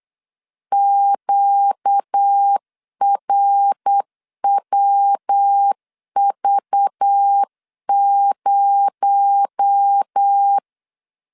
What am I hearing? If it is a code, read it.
Morse code, heard as a keyed tone, about 8 wpm, 800 Hz. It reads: QRWV0
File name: CW / DET / 1 9 1500C S A 8wpm 5637